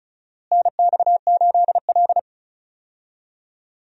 Morse code, heard as a keyed tone, about 36 wpm, 700 Hz. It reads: NX8L
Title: USA Call Signs - 4 Characters - Repeated 2x - 35wpm